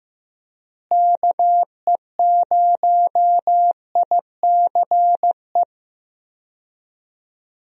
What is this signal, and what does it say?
Morse code, heard as a keyed tone, about 15 wpm, 700 Hz. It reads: KE0ICE
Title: USA Call Signs - 6 Characters - Repeated 3x - 15wpm